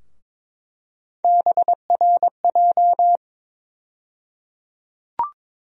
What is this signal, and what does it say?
Morse code, heard as a keyed tone, about 22 wpm, 700 Hz. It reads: BRJ
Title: USA Call Sign Postfixes - 3 Letters 22wpm